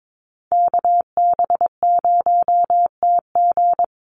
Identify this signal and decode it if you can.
Morse code, heard as a keyed tone, about 22 wpm, 700 Hz. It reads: KB0TG
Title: USA Call Signs Repeated 3x 22wpm